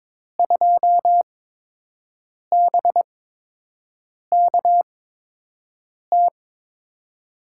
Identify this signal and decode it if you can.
Morse code, heard as a keyed tone, about 22 wpm, spaced as for 6 wpm, 700 Hz. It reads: 2BKT
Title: USA Call Signs - 8x Character Spacing - 22wpm